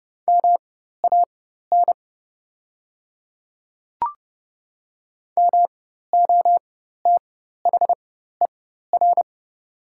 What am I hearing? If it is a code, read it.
Morse code, heard as a keyed tone, about 30 wpm, spaced as for 14 wpm, 700 Hz. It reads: MAN MOTHER
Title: 4x Character Spacing - Top 200 Words - 30wpm